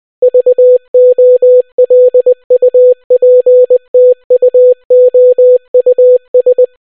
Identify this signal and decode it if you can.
Morse code, heard as a keyed tone, about 20 wpm, 500 Hz. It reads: VOLUPTUOUS